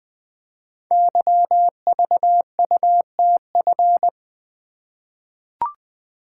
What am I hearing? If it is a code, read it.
Morse code, heard as a keyed tone, about 20 wpm, 700 Hz. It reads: YVUTF
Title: Code Groups - 5 Letter:Numbers - 20wpm